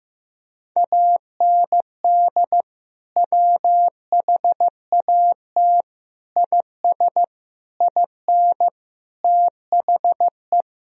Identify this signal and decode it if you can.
Morse code, heard as a keyed tone, about 15 wpm, 700 Hz. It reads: AND WHAT IS IN THE
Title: Sentences from Top 1000 Words - Review Part 1 - 15wpm